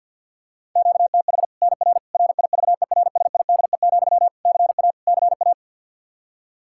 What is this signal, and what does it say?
Morse code, heard as a keyed tone, about 51 wpm, 700 Hz. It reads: QTH NR RIVERSIDE, CA CA
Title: Practice QSO - Exchange 1 - Beginning - 50wpm